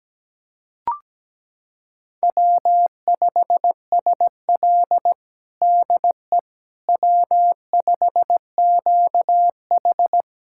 Morse code, heard as a keyed tone, about 17 wpm, 700 Hz. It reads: W5SL DE W5QH
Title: USA Callsign DE Callsign - 4 Characters - 17wpm